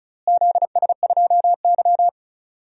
Morse code, heard as a keyed tone, about 34 wpm, 700 Hz. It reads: ZS2Y
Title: DXCC-100 Call Signs - 4 Characters - Repeated 2x - 35wpm